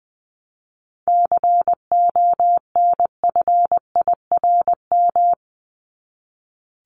Morse code, heard as a keyed tone, about 20 wpm, 700 Hz. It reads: CONFIRM
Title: Top 1751-2000 Words 20wpm